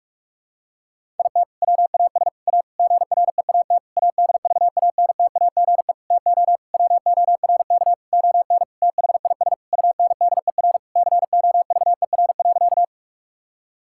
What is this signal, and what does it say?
Morse code, heard as a keyed tone, about 41 wpm, 700 Hz. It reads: IT WAS A GREAT ADVANTAGE TO WORK ON THIS UNDER COVER.
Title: Sentences from Top 1000 Words - Review Part 6 - 45wpm